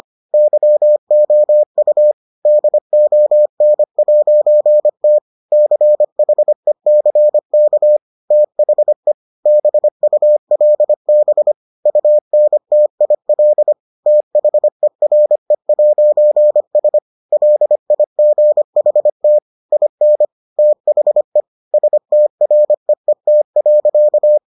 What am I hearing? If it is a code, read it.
Morse code, heard as a keyed tone, about 25 wpm, 600 Hz. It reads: YOU DON'T CHECK THE BULB UNTIL THERE'S LIGHT IN THE STREET.